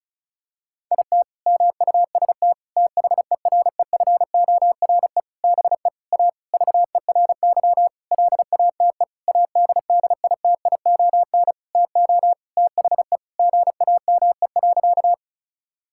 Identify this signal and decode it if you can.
Morse code, heard as a keyed tone, about 36 wpm, 700 Hz. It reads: IT MUST THEREFORE BE A VERY LATE ADDITION TO THE GAME.
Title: Sentences from Top 1000 Words - Review Part 6 - 35wpm